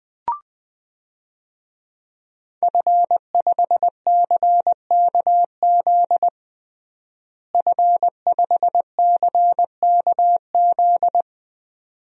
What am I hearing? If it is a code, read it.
Morse code, heard as a keyed tone, about 20 wpm, 700 Hz. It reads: F5CKZ F5CKZ
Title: DXCC-100 Call Signs - 5 Characters - Repeated 2x - 20wpm